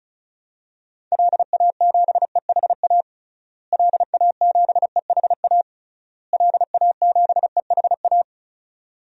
Morse code, heard as a keyed tone, about 36 wpm, 700 Hz. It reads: LA7EHA LA7EHA LA7EHA
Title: DXCC-100 Call Signs - 6 Characters - Repeated 3x - 35wpm